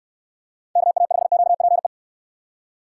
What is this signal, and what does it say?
Morse code, heard as a keyed tone, about 63 wpm, 700 Hz. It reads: KA5QPE